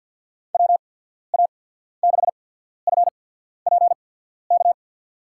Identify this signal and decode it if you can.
Morse code, heard as a keyed tone, about 53 wpm, spaced as for 14 wpm, 700 Hz. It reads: WA6FPK
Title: USA Call Signs - 6 Characters - 8x Character Spacing - 50wpm